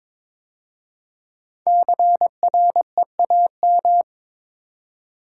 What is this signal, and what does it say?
Morse code, heard as a keyed tone, about 22 wpm, 700 Hz. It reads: CREAM
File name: 5 Letter Words - First 3 Letters Given - 22wpm